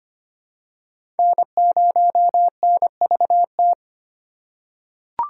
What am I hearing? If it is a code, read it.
Morse code, heard as a keyed tone, about 25 wpm, 700 Hz. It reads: N0NVT
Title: USA Call Signs - 5 Characters - 4x Character Spacing - 25wpm